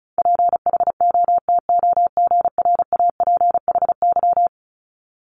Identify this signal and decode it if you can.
Morse code, heard as a keyed tone, about 36 wpm, 700 Hz. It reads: PHOTOGRAPHY
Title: Multi-letter Phonemes 35wpm